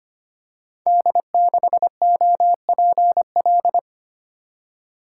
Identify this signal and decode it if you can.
Morse code, heard as a keyed tone, about 25 wpm, 700 Hz. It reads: D6OPL
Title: Code Groups - 5 Letter:Numbers - 25wpm